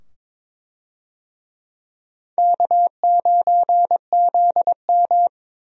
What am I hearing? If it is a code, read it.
Morse code, heard as a keyed tone, about 22 wpm, 700 Hz. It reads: K9ZM
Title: USA Call Signs - 4 Characters - 22wpm